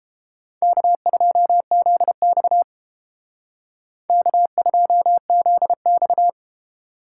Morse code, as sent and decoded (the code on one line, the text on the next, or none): -.- ..--- --.. -..- / -.- ..--- --.. -..-
K2ZX K2ZX